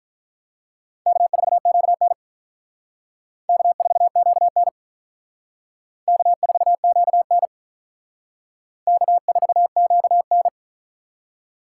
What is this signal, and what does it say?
Morse code, heard as a keyed tone, about 44 wpm, 700 Hz. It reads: K4QN K4QN K4QN K4QN